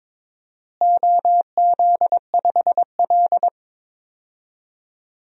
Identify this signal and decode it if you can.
Morse code, heard as a keyed tone, about 22 wpm, 700 Hz. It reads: OZ5L
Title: DXCC-100 Call Signs - 4 Characters - Repeated 3x - 22wpm